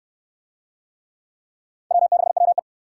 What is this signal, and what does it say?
Morse code, heard as a keyed tone, about 68 wpm, 700 Hz. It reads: W6WE